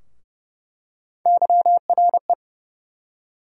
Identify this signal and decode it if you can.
Morse code, heard as a keyed tone, about 30 wpm, 700 Hz. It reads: YRE
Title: USA Call Sign Postfixes - 3 Letters 30wpm